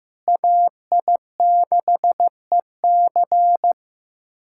Morse code, heard as a keyed tone, about 16 wpm, 700 Hz. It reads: AI6EC
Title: USA Callsign DE Callsign - 5 Characters - 15wpm